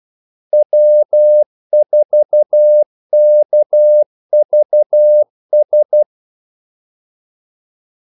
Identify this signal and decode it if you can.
Morse code, heard as a keyed tone, about 12 wpm, 600 Hz. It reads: W4KVS
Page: 2 ekstra ordmellemrum mellem kaldesignalerne